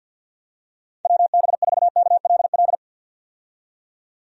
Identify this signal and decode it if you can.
Morse code, heard as a keyed tone, about 52 wpm, 700 Hz. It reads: WB4KLL
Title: USA Call Signs - 6 Characters - 50wpm